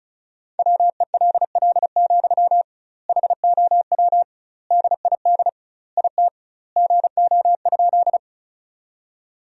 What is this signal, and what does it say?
Morse code, heard as a keyed tone, about 34 wpm, 700 Hz. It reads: WELL, HOW DID IT GO?
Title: 2x Word-Spacing - Sentences from Top 200 Words - 35wpm